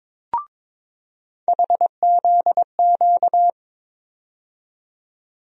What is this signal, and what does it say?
Morse code, heard as a keyed tone, about 22 wpm, 700 Hz. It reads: HZQ
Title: USA Call Sign Postfixes - 3 Letters 22wpm